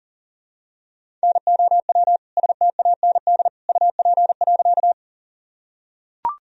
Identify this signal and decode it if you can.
Morse code, heard as a keyed tone, about 40 wpm, 700 Hz. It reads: NOW STAND UP.
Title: Sentences from Top 1000 Words - Review Part 1 - 40wpm